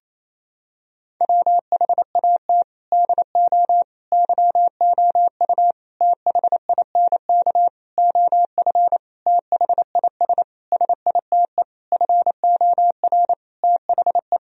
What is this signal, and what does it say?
Morse code, heard as a keyed tone, about 28 wpm, 700 Hz. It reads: WHAT DO YOU THINK OF THIS SITE FOR THE
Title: Sentences from Top 1000 Words - Review Part 1 - 28wpm